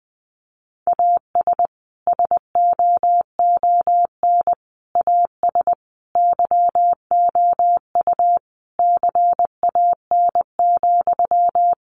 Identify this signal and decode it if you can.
Morse code, heard as a keyed tone, about 20 wpm, 700 Hz. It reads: AS SOON AS YOU CAN,
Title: Sentences from Top 1500 Words 20wpm